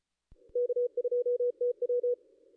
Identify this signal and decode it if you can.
Morse code, heard as a keyed tone, about 34 wpm, 440 Hz. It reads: K2TW
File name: contests / ss-cw-k5tr-op-n5rz -2200